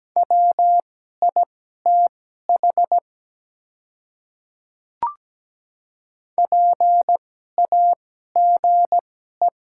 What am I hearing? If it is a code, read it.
Morse code, heard as a keyed tone, about 17 wpm, spaced as for 12 wpm, 700 Hz. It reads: WITH PAGE
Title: 2x Character Spacing - Top 300 Words - 17wpm